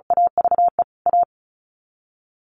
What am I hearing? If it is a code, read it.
Morse code, heard as a keyed tone, about 34 wpm, 700 Hz. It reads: AVE A
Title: Sets of 5 Words Encore 35wpm